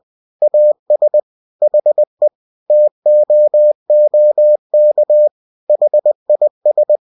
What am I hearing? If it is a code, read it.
Morse code, heard as a keyed tone, about 20 wpm, 600 Hz. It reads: AS HE TOOK HIS